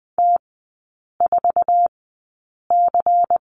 Morse code, heard as a keyed tone, about 20 wpm, spaced as for 8 wpm, 700 Hz. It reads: T4C
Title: 093 Lesson 16e - Letter c - Three Characters 20wpm